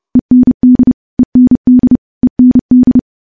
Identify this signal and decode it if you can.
Morse code, heard as a keyed tone, about 30 wpm, 250 Hz. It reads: END END END
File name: morse / morse-training-30WPM